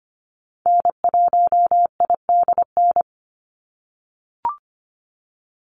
Morse code, heard as a keyed tone, about 25 wpm, 700 Hz. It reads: N1IDN
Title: NA SOTA Activator Callsigns 25wpm